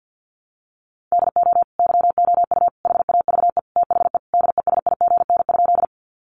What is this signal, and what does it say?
Morse code, heard as a keyed tone, about 50 wpm, 700 Hz. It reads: DO YOU HAVE THE DESIGN?
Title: Sentences from Top 800 Words 50wpm